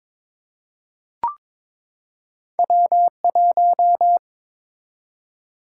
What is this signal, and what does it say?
Morse code, heard as a keyed tone, about 23 wpm, 700 Hz. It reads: W1
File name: practice / 22wpm / Two Letter-Number Combinations - 22wpm